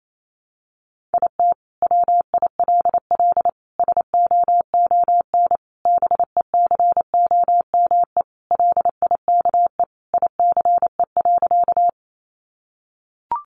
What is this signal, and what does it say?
Morse code, heard as a keyed tone, about 28 wpm, 700 Hz. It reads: IT WILL SOON BECOME LIKE ICE.
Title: Sentences from Top 1000 Words 28wpm